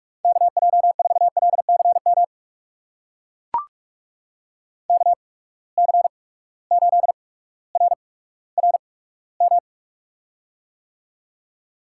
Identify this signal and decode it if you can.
Morse code, heard as a keyed tone, about 43 wpm, spaced as for 19 wpm, 700 Hz. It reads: KJ4LCM KC8RRM
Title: USA Call Signs - 6 Characters - 8x Character Spacing - 45wpm